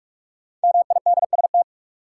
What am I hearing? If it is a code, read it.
Morse code, heard as a keyed tone, about 46 wpm, 700 Hz. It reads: MIDST
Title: Sets of 4 Words - New Word Drill - 45wpm